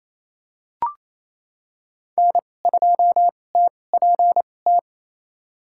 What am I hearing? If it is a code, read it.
Morse code, heard as a keyed tone, about 28 wpm, spaced as for 20 wpm, 700 Hz. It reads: N2TPT
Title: USA Call Signs - 5 Characters - 2x Character Spacing - 28wpm